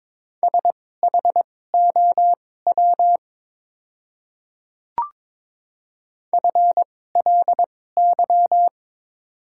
Morse code, heard as a keyed tone, about 22 wpm, spaced as for 16 wpm, 700 Hz. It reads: SHOW FLY
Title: 2x Character Spacing - Top 500 Words - 22wpm